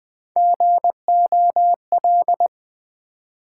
Morse code, heard as a keyed tone, about 20 wpm, 700 Hz. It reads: GOL
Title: Build A Word - 1 Letter At A Time - 3 to 5-Letter Words - 20wpm